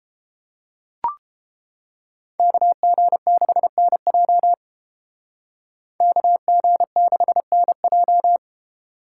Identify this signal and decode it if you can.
Morse code, heard as a keyed tone, about 29 wpm, 700 Hz. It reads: KG6NJ KG6NJ